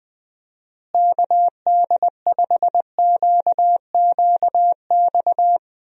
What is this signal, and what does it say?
Morse code, heard as a keyed tone, about 20 wpm, 700 Hz. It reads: KD5QQX
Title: USA Call Signs - 6 Characters - Repeated 2x - 20wpm